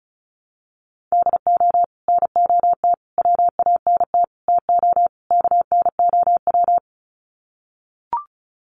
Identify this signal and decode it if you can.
Morse code, heard as a keyed tone, about 34 wpm, 700 Hz. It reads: DO NOT WANT TO KNOW
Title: Sets of 5 Words Encore 35wpm